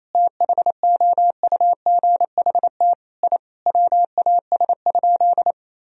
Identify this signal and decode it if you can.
Morse code, heard as a keyed tone, about 28 wpm, 700 Hz. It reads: THOUGHT I WAS?
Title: Sentences from Top 1000 Words - Review Part 3 - 28wpm